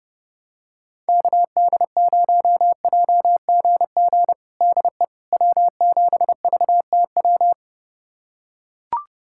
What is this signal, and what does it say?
Morse code, heard as a keyed tone, about 30 wpm, 700 Hz. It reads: KD0JGG DE W7VTW
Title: USA Callsign DE Callsign 30wpm